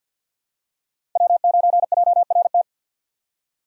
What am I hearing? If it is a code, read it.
Morse code, heard as a keyed tone, about 47 wpm, 700 Hz. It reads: W9JRT
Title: USA Call Signs - 4x Character Spacing - 50wpm